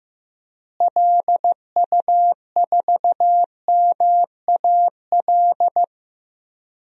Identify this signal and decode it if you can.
Morse code, heard as a keyed tone, about 15 wpm, 700 Hz. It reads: LU4MAL
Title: DXCC-100 Call Signs - 6 Characters - Repeated 3x - 15wpm